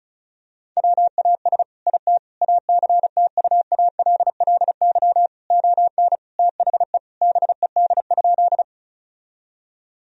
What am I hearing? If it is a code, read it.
Morse code, heard as a keyed tone, about 36 wpm, 700 Hz. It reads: WAS IT ACTUALLY ON THE BED?
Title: Sentences from Top 1000 Words - Review Part 4 - 35wpm